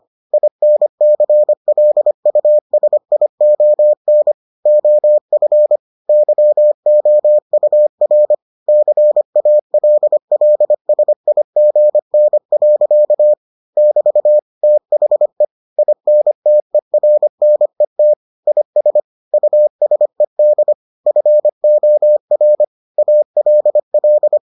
Morse code, heard as a keyed tone, about 25 wpm, 600 Hz. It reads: INCLUSION OF YOUR CALLSIGN. <BT> THE INTERNET IS USED FOR ALL